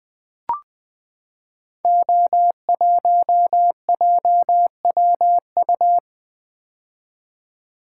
Morse code, heard as a keyed tone, about 20 wpm, 700 Hz. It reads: O1JWU
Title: Code Groups - 5 Letter:Numbers - 20wpm